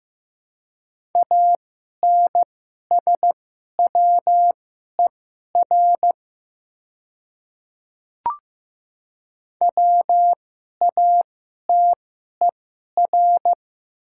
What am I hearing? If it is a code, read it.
Morse code, heard as a keyed tone, about 15 wpm, spaced as for 11 wpm, 700 Hz. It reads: ANSWER WATER
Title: 2x Character Spacing - Top 300 Words - 15wpm